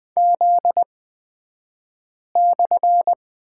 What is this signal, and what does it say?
Morse code, heard as a keyed tone, about 20 wpm, spaced as for 5 wpm, 700 Hz. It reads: Z/
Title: 242 Lesson 39d - Letter z - Two Characters 20wpm